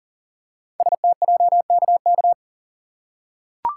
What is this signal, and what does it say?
Morse code, heard as a keyed tone, about 40 wpm, 700 Hz. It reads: STJKK
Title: Code Groups - 5 Letters - 40wpm